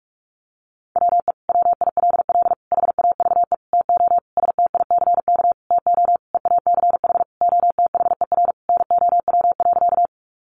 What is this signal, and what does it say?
Morse code, heard as a keyed tone, about 44 wpm, 700 Hz. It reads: WE WILL HAVE TO STICK TO EACH OTHER NOW.
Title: Sentences from Top 1200 Words 45wpm